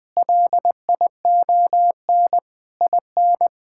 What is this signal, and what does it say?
Morse code, heard as a keyed tone, about 21 wpm, 700 Hz. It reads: LION IN
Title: Difficult Sentences with Hints - 20wpm